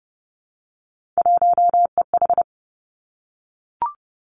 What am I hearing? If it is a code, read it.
Morse code, heard as a keyed tone, about 30 wpm, 700 Hz. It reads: 1EH
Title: Three Challenging Letter-Number Combinations 30wpm